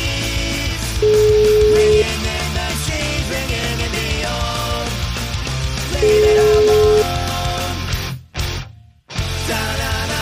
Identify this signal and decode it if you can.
Ringback tone (Europe, 425 Hz), with noise.